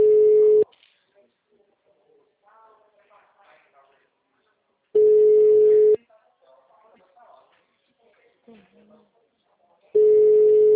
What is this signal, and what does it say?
Ringback tone (Europe, 425 Hz), with speech.
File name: out- -2007- - - .59102